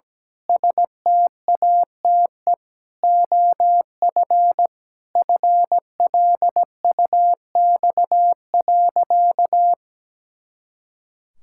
Morse code, heard as a keyed tone, about 17 wpm, 700 Hz. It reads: STATE OF FLUX.
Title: Pangram Challenge v2 - 17wpm